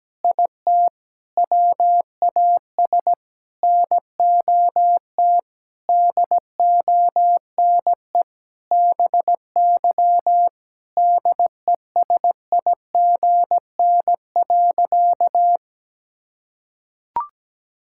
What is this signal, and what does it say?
Morse code, heard as a keyed tone, about 17 wpm, 700 Hz. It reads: IT WAS NOT DONE BY DESIGN.
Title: Sentences from Top 1000 Words - Review Part 5 - 17wpm